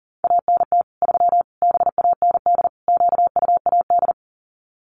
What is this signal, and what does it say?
Morse code, heard as a keyed tone, about 40 wpm, 700 Hz. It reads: ANT 3 BAND QUAD
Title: Common QSO Phrases 40wpm